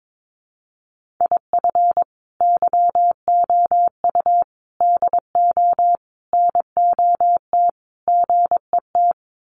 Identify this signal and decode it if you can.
Morse code, heard as a keyed tone, about 22 wpm, 700 Hz. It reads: IF YOU DO NOT GET
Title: Sets of 5 Words Encore2 22wpm